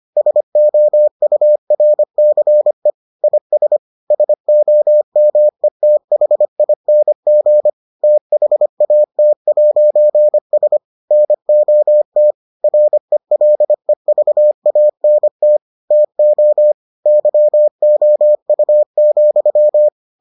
Morse code, heard as a keyed tone, about 25 wpm, 600 Hz. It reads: SOURCE IS SOMETHING THAT'S NOT RELEVANT TO YOU,